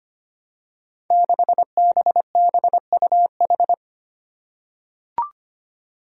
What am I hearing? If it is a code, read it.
Morse code, heard as a keyed tone, about 25 wpm, 700 Hz. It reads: 6BBUH
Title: Five Challenging Letter-Number Combinations 25wpm